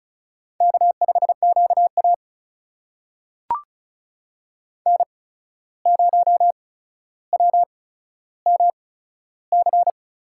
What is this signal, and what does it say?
Morse code, heard as a keyed tone, about 36 wpm, spaced as for 14 wpm, 700 Hz. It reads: K5QA N0WMC
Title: USA Call Signs - 8x Character Spacing - 35wpm